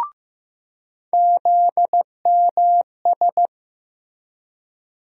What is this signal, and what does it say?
Morse code, heard as a keyed tone, about 15 wpm, 700 Hz. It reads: ZMS
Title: USA Call Sign Postfixes - 3 Letters 15wpm